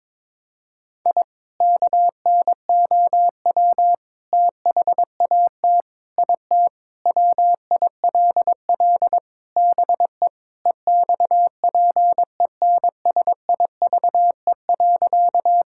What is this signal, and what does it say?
Morse code, heard as a keyed tone, about 21 wpm, 700 Hz. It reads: I KNOW THAT IT WILL BE EXPENSIVE.
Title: Sentences from Top 1000 Words - Review Part 6 - 22wpm